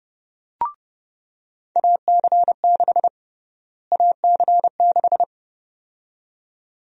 Morse code, heard as a keyed tone, about 31 wpm, 700 Hz. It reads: AC6 AC6
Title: USA Call Sign Prefixes - Repeated 2x - 30wpm